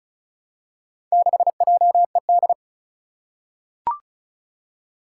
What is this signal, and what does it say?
Morse code, heard as a keyed tone, about 36 wpm, 700 Hz. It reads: 6JED
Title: Four Challenging Letter-Number Combinations 35wpm